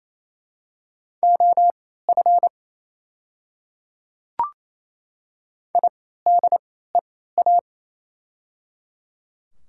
Morse code, heard as a keyed tone, about 28 wpm, spaced as for 16 wpm, 700 Hz. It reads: OF IDEA
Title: 3x Character Spacing - Top 300 Words - 28wpm